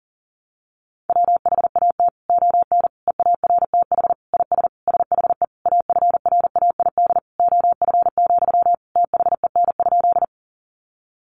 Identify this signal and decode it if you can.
Morse code, heard as a keyed tone, about 40 wpm, 700 Hz. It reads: WHAT ON EARTH IS SHE AFRAID OF, THEN?
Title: Sentences from Top 1500 Words 40wpm